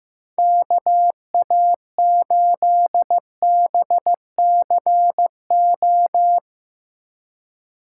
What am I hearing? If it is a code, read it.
Morse code, heard as a keyed tone, about 15 wpm, 700 Hz. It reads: KA8BCO
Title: USA Call Signs - 2x Character Spacing - 15wpm